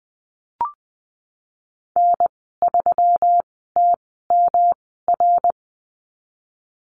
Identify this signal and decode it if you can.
Morse code, heard as a keyed tone, about 20 wpm, spaced as for 14 wpm, 700 Hz. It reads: N3TMR
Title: 161 Lesson 26g - Letter g - Call Signs 20wpm